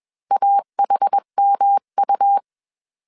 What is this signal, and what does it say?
Morse code, heard as a keyed tone, about 21 wpm, 800 Hz. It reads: AHMU
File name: CW / DET / 1 3 1500C S B 20wpm 1754